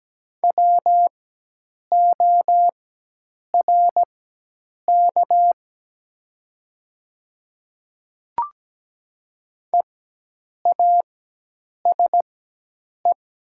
Morse code, heard as a keyed tone, about 17 wpm, spaced as for 8 wpm, 700 Hz. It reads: WORK EASE
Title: 4x Character Spacing - Top 300 Words - 17wpm